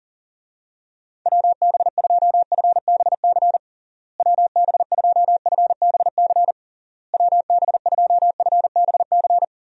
Morse code, heard as a keyed tone, about 40 wpm, 700 Hz. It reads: WB2FBC WB2FBC WB2FBC
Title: USA Call Signs - 6 Characters - Repeated 3x - 40wpm